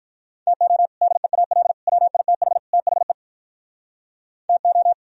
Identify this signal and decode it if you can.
Morse code, heard as a keyed tone, about 46 wpm, 700 Hz. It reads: TO DEAL WITH THE TO